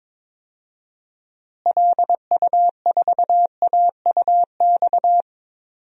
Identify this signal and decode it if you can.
Morse code, heard as a keyed tone, about 22 wpm, 700 Hz. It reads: LU4AUX
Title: DXCC-100 Call Signs - 6 Characters - Repeated 2x - 22wpm